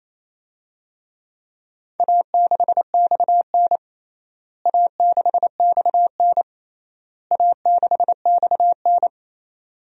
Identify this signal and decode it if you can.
Morse code, heard as a keyed tone, about 27 wpm, 700 Hz. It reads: A6XN A6XN A6XN
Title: DXCC-100 Call Signs - 4 Characters - Repeated 3x - 28wpm